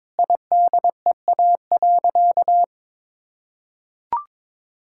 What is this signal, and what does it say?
Morse code, heard as a keyed tone, about 22 wpm, 700 Hz. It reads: IDEA.
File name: practice / 22wpm / Sentences with 2-Word Phrases - 22wpm